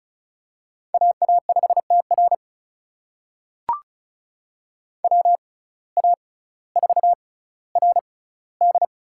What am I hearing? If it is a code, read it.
Morse code, heard as a keyed tone, about 36 wpm, spaced as for 18 wpm, 700 Hz. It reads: AA5TR WA4RD
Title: USA Call Signs - 6x Character Spacing - 35wpm